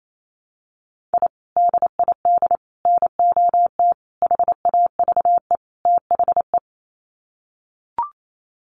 Morse code, heard as a keyed tone, about 28 wpm, 690 Hz. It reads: I DID NOT HAVE THE
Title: Sets of 5 Words Encore 28wpm